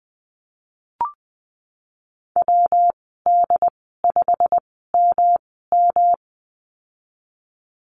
Morse code, heard as a keyed tone, about 20 wpm, spaced as for 15 wpm, 700 Hz. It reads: WD5MM
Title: 122 Lesson 20g - Number 3 - Call Signs 20wpm